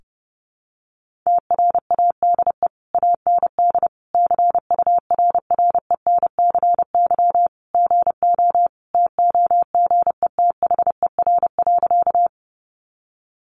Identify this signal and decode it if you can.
Morse code, heard as a keyed tone, about 30 wpm, 700 Hz. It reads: TRADE AND CURRENCY GO TOGETHER.
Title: Sentences from Top 1500 Words 30wpm